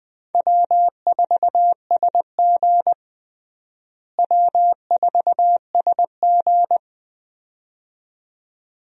Morse code, heard as a keyed tone, about 20 wpm, 700 Hz. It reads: W4SG W4SG
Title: USA Call Signs - 4 Characters - Repeated 2x - 20wpm